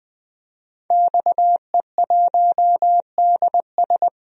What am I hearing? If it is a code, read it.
Morse code, heard as a keyed tone, about 21 wpm, 700 Hz. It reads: XE1DS
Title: DXCC-100 Call Signs - 5 Characters - 20wpm